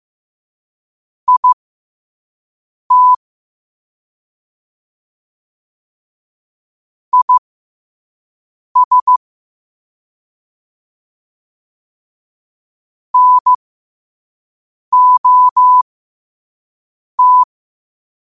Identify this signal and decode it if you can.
Morse code, heard as a keyed tone, about 15 wpm, spaced as for 5 wpm, 1000 Hz. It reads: IT IS NOT
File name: qotdfiles / 5wpm